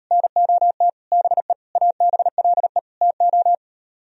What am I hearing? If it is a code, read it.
Morse code, heard as a keyed tone, about 38 wpm, 700 Hz. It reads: NOT BE ABLE TO